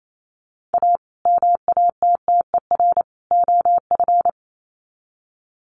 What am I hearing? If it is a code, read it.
Morse code, heard as a keyed tone, about 28 wpm, 700 Hz. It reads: A MATTER OF
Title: Sets of 3 Words 28wpm